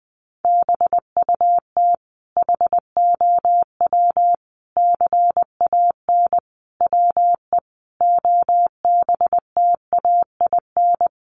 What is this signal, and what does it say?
Morse code, heard as a keyed tone, about 20 wpm, 700 Hz. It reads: BUT HOW CAN WE OBTAIN
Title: Sentences from Top 1300 Words 20wpm